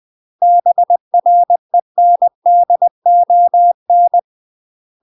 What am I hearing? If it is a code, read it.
Morse code, heard as a keyed tone, about 20 wpm, 700 Hz. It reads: BRENDON
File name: UK MASTER QSOs TRAINING / 20WPM / UK MASTER QSOs 0066